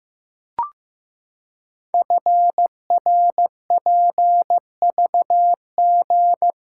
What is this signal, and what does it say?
Morse code, heard as a keyed tone, about 15 wpm, 700 Hz. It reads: FRPVG
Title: Code Groups - 5 Letters - 15wpm